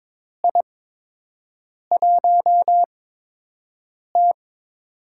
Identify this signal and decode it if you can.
Morse code, heard as a keyed tone, about 22 wpm, spaced as for 6 wpm, 700 Hz. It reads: I1T
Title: USA Call Signs - 4 Characters - 8x Character Spacing - 22wpm